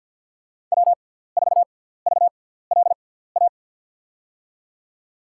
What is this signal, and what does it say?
Morse code, heard as a keyed tone, about 51 wpm, spaced as for 17 wpm, 700 Hz. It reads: W4VLA
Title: USA Call Signs - 5 Characters - 6x Character Spacing - 50wpm